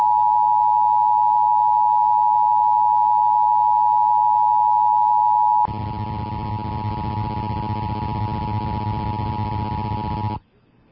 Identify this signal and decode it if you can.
Emergency Alert System (SAME) tones (the 853 and 960 Hz attention tone). It is heard from 0.8 to 9.9 s.